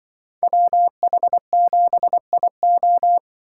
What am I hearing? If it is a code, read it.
Morse code, heard as a keyed tone, about 24 wpm, 700 Hz. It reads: WH7IO
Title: Speed Racing - USA 2x2 Call Signs 20wpm